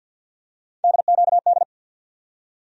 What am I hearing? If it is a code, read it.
Morse code, heard as a keyed tone, about 47 wpm, 700 Hz. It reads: DQD
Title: USA Call Sign Postfixes - 3 Letters 50wpm